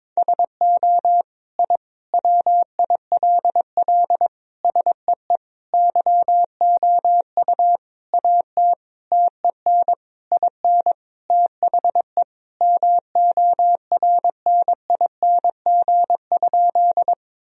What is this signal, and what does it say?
Morse code, heard as a keyed tone, about 22 wpm, 700 Hz. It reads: SO I WILL SEE YOU AT TEN IN THE MORNING?
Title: Sentences from Top 1000 Words - Review Part 2 - 22wpm